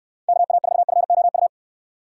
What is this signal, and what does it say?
Morse code, heard as a keyed tone, about 63 wpm, 700 Hz. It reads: KA4FOU